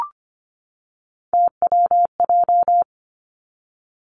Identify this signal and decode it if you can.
Morse code, heard as a keyed tone, about 25 wpm, 700 Hz. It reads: TWJ
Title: Three Challenging Letter-Number Combinations 25wpm